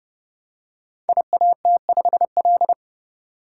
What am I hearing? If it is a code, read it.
Morse code, heard as a keyed tone, about 30 wpm, 700 Hz. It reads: IAT5L
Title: Code Groups - 5 Letter:Numbers - 30wpm